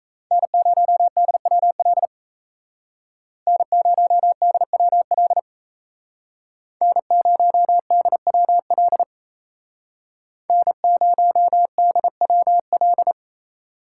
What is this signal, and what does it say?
Morse code, heard as a keyed tone, about 36 wpm, 700 Hz. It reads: N0DWL N0DWL N0DWL N0DWL